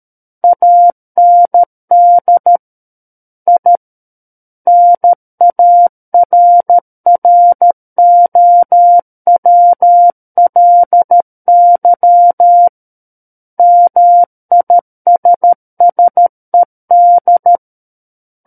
Morse code, heard as a keyed tone, about 13 wpm, 700 Hz. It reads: AND I NARROWLY MISSED
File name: Book One / 04-Chapter 4 - 13 WPM